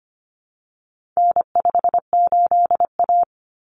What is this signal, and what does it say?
Morse code, heard as a keyed tone, about 25 wpm, 700 Hz. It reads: N58A
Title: Common Cut Numbers - 4-Digit 25wpm